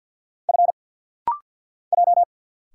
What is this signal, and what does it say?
Morse code, heard as a keyed tone, about 48 wpm, 700 Hz. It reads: F J
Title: Single Letters - Rapid-Fire 50wpm